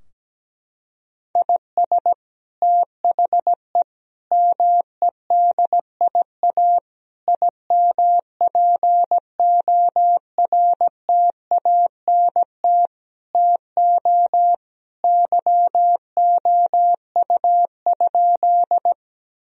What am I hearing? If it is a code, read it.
Morse code, heard as a keyed tone, about 17 wpm, 700 Hz. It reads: IS THE MEDIA IMPORTANT TO YOU?
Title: Sentences from Top 1000 Words - Review Part 4 - 17wpm